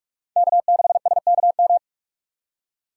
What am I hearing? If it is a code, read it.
Morse code, heard as a keyed tone, about 58 wpm, 700 Hz. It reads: K6SKM